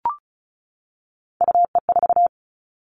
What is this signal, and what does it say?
Morse code, heard as a keyed tone, about 36 wpm, 700 Hz. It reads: UE4
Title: Three Challenging Letter-Number Combinations 35wpm